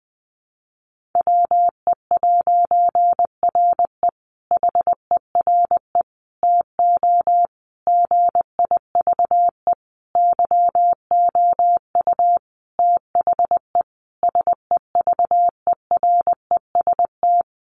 Morse code, heard as a keyed tone, about 20 wpm, 700 Hz. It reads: WE'RE HERE TO GIVE YOU THE SEVEREST
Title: A Sound Of Thunder 20wpm